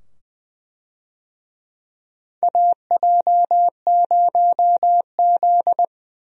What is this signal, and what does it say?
Morse code, heard as a keyed tone, about 20 wpm, 700 Hz. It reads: AJ0Z
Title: USA Call Signs - 4 Characters - 20wpm